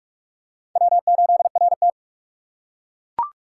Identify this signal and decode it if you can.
Morse code, heard as a keyed tone, about 43 wpm, 700 Hz. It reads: W8RT